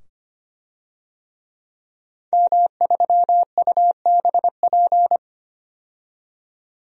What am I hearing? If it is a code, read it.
Morse code, heard as a keyed tone, about 25 wpm, 700 Hz. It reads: M3UBP
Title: DXCC-100 Call Signs - 5 Characters - 25wpm